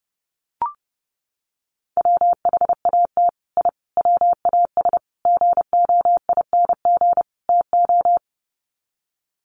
Morse code, heard as a keyed tone, about 30 wpm, 700 Hz. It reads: WHAT I WAS GOING TO
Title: Sets of 5 Words Encore2 30wpm